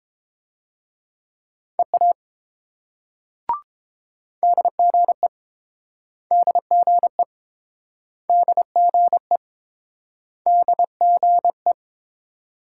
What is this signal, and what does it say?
Morse code, heard as a keyed tone, about 31 wpm, 700 Hz. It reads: EA DGE DGE DGE DGE